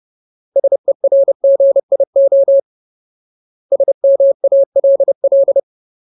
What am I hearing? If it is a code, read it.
Morse code, heard as a keyed tone, about 30 wpm, 550 Hz. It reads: SERGIO SMALL
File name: Ultimate Word List / 30WPM Ultimate